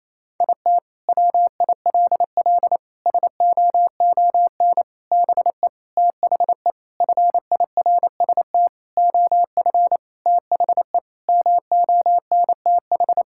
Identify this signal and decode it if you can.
Morse code, heard as a keyed tone, about 28 wpm, 700 Hz. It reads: IT WILL SOON BE THE FIRST OF THE MONTH
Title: Sentences from Top 1000 Words - Review Part 5 - 28wpm